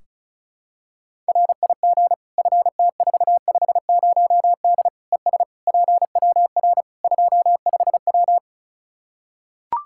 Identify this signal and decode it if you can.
Morse code, heard as a keyed tone, about 36 wpm, 700 Hz. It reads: RIG FT450D ES PWR 25W
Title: K1USN SST QSO Practice 35wpm